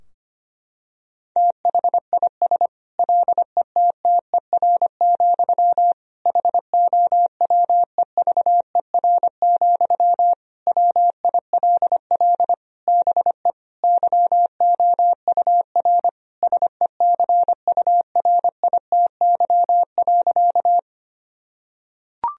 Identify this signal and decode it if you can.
Morse code, heard as a keyed tone, about 25 wpm, 700 Hz. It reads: THIS LETTER, HOWEVER, WILL BE YOUR SECURITY.
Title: Sentences from Top 1000 Words - Review Part 4 - 25wpm